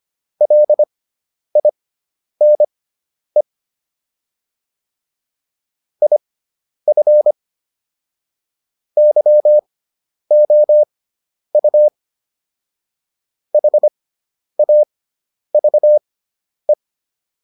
Morse code, heard as a keyed tone, about 25 wpm, spaced as for 10 wpm, 600 Hz. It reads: LINE IF YOU HAVE